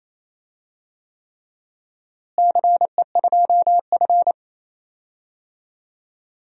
Morse code, heard as a keyed tone, about 28 wpm, 700 Hz. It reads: CE2F
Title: DXCC-100 Call Signs - 4 Characters - Repeated 3x - 28wpm